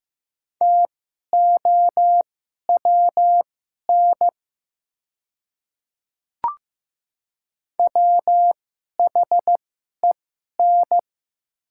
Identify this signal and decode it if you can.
Morse code, heard as a keyed tone, about 15 wpm, spaced as for 11 wpm, 700 Hz. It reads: TOWN WHEN
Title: 2x Character Spacing - Top 500 Words - 15wpm